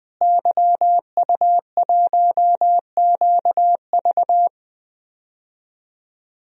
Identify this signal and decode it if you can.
Morse code, heard as a keyed tone, about 20 wpm, 700 Hz. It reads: YU1QV
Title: DXCC-100 Call Signs - 5 Characters - Repeated 2x - 20wpm